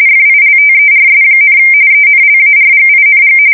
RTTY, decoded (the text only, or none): FILTERS, ENLARGING